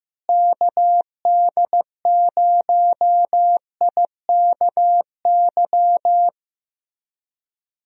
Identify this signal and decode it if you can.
Morse code, heard as a keyed tone, about 15 wpm, 700 Hz. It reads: KD0IKY
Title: USA Call Signs - 6 Characters - Repeated 3x - 15wpm